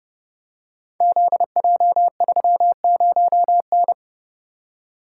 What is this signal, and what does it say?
Morse code, heard as a keyed tone, about 30 wpm, 700 Hz. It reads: ZJ30N
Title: Code Groups - 5 Letter:Numbers - 30wpm